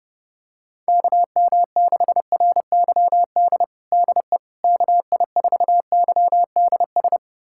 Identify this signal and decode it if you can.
Morse code, heard as a keyed tone, about 30 wpm, 700 Hz. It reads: KM6RYD DE KI4YDS
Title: USA Callsign DE Callsign - 6 Characters - 30wpm